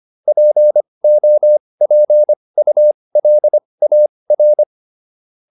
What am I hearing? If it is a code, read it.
Morse code, heard as a keyed tone, about 25 wpm, spaced as for 20 wpm, 600 Hz. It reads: POPULAR
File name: SS315 20